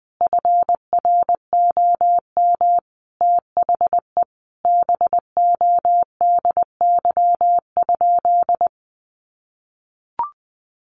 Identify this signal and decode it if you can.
Morse code, heard as a keyed tone, about 20 wpm, 700 Hz. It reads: FROM THE BODY?
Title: Sentences from Top 400 Words 20wpm